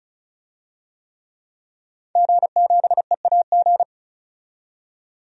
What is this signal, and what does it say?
Morse code, heard as a keyed tone, about 36 wpm, 700 Hz. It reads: G7EAG
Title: DXCC-100 Call Signs - 5 Characters - 35wpm